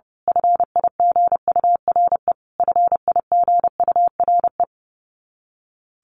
Morse code, heard as a keyed tone, about 30 wpm, 700 Hz. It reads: FIGURE FIGURE
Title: Top 500 Words Repeated 30wpm